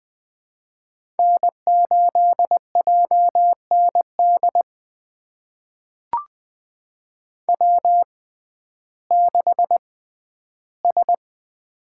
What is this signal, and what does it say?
Morse code, heard as a keyed tone, about 20 wpm, spaced as for 12 wpm, 700 Hz. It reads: N8JND W6S
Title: USA Call Signs - 5 Characters - 6x Character Spacing - 20wpm